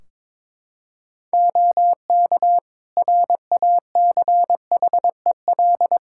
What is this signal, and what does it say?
Morse code, heard as a keyed tone, about 22 wpm, 700 Hz. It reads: OK RACHEL
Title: K1USN SST QSO Practice 22wpm